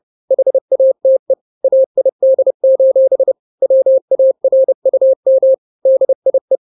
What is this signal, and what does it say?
Morse code, heard as a keyed tone, about 30 wpm, 530 Hz. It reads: HATE AID: WARUM DIE
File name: 29wpm